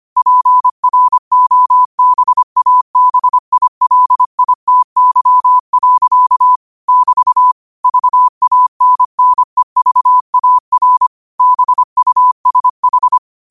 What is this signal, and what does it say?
Morse code, heard as a keyed tone, about 25 wpm, 1000 Hz. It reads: PROBABILITY. <BT> VANNEVAR BUSH
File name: qotdfiles / 25wpm